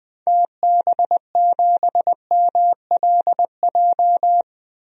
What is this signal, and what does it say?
Morse code, heard as a keyed tone, about 21 wpm, 700 Hz. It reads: TB7MLJ
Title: DXCC-100 Call Signs - 6 Characters - 20wpm